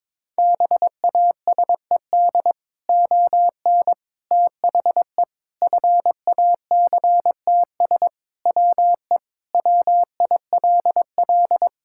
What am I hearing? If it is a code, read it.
Morse code, heard as a keyed tone, about 22 wpm, 700 Hz. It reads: BASED ON THE FACTS WE WILL
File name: practice / 22wpm / Sentences with 3-Word Phrases - 22wpm